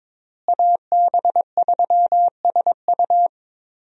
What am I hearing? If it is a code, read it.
Morse code, heard as a keyed tone, about 22 wpm, 700 Hz. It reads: AB3SU
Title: USA Call Signs - 5 Characters - Repeated 3x - 22wpm